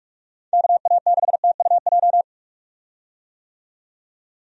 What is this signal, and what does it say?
Morse code, heard as a keyed tone, about 45 wpm, 700 Hz. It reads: KA6TUJ
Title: USA Call Signs - 6 Characters - 45wpm